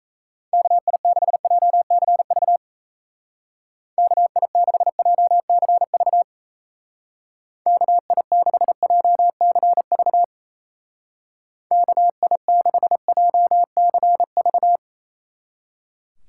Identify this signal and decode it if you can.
Morse code, heard as a keyed tone, about 37 wpm, 700 Hz. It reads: KI6JCV KI6JCV KI6JCV KI6JCV